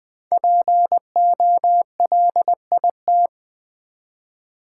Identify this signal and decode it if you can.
Morse code, heard as a keyed tone, about 20 wpm, 700 Hz. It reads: POLIT
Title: Build A Word - 5 Letters At A Time - 10-Letter Words - 20wpm